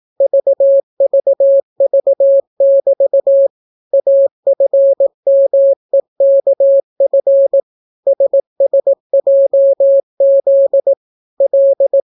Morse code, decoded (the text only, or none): VVV<BT> AFMEKF SSJZ L